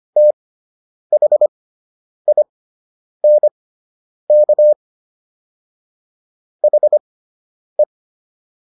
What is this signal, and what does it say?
Morse code, heard as a keyed tone, about 25 wpm, spaced as for 9 wpm, 600 Hz. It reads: THINK HE